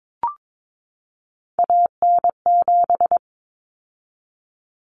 Morse code, heard as a keyed tone, about 22 wpm, 700 Hz. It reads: AN7
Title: Common Cut Numbers - 3-Digit 22wpm